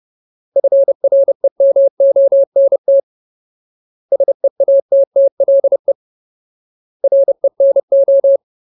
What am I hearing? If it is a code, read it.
Morse code, heard as a keyed tone, about 30 wpm, 550 Hz. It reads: FREMONT SEATTLE RENO
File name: Ultimate Word List / 30WPM Ultimate